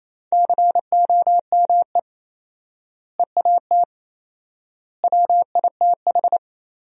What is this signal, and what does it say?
Morse code, heard as a keyed tone, about 28 wpm, 700 Hz. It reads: COME EAT WITH
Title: 4x Word-Spacing - Sentences from Top 300 Words - 28wpm